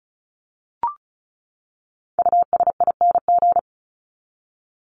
Morse code, heard as a keyed tone, about 34 wpm, 700 Hz. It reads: USING
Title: ING Suffix - Top 100 Words with Base Words 35wpm